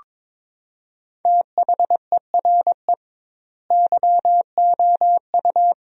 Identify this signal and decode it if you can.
Morse code, heard as a keyed tone, about 22 wpm, 700 Hz. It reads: THERE YOU
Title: 2x Word-Spacing - Sentences from Top 100 Words - 22wpm